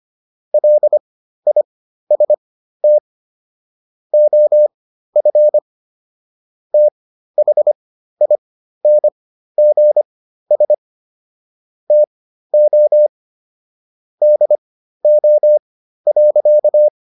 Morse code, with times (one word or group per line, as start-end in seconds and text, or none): 0.5-3.0 LIST
4.1-5.6 OF
6.7-10.7 THINGS
11.9-13.1 TO
14.2-16.9 DO.